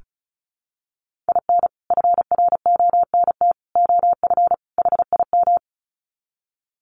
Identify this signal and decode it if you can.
Morse code, heard as a keyed tone, about 34 wpm, 700 Hz. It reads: IN FRONT OF HIM
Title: Sets of 4 Words 35wpm